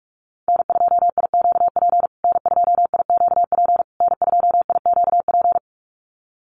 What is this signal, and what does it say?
Morse code, heard as a keyed tone, about 44 wpm, 700 Hz. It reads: N2IQP N2IQP N2IQP
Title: USA Call Signs Repeated 3x 45wpm